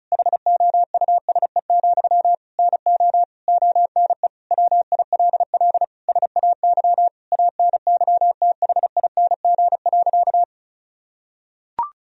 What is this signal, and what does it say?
Morse code, heard as a keyed tone, about 36 wpm, 700 Hz. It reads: HOUSE, NO ONE WILL SAY ANYTHING.
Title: Sentences from Top 1000 Words - Review Part 2 - 35wpm